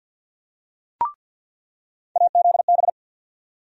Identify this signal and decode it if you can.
Morse code, heard as a keyed tone, about 47 wpm, 700 Hz. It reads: AZB
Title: USA Call Sign Postfixes - 3 Letters 50wpm